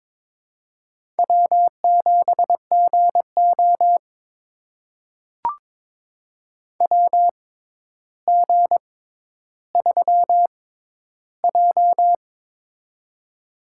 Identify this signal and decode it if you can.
Morse code, heard as a keyed tone, about 22 wpm, spaced as for 11 wpm, 700 Hz. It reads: W7GO WG3J
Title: USA Call Signs - 6x Character Spacing - 22wpm